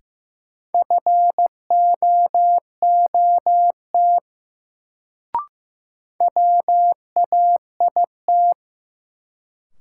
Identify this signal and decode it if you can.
Morse code, heard as a keyed tone, about 15 wpm, 700 Hz. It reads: FOOT WAIT
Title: Sentences from Top 500 Words - New Word Drill - 15wpm